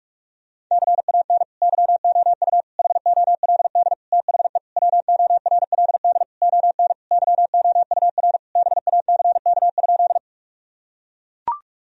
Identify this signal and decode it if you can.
Morse code, heard as a keyed tone, about 42 wpm, 700 Hz. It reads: CAN YOU HOLD THE WORLD ON YOUR BACK?
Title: Sentences from Top 1000 Words - Review Part 2 - 45wpm